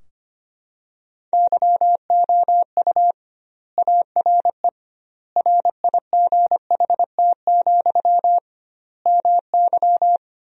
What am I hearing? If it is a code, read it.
Morse code, heard as a keyed tone, about 25 wpm, 700 Hz. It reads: YOU ARE RIGHT, MY
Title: 2x Word-Spacing - Sentences from Top 400 Words - 25wpm